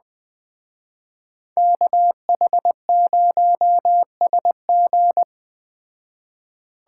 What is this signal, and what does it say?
Morse code, heard as a keyed tone, about 20 wpm, 700 Hz. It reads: KH0SG
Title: Speed Racing - USA 2x2 Call Signs 17wpm